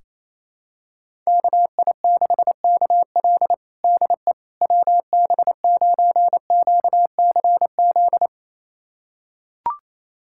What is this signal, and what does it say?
Morse code, heard as a keyed tone, about 28 wpm, 700 Hz. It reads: KI6KL DE WB9QCZ
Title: USA Callsign DE Callsign 28wpm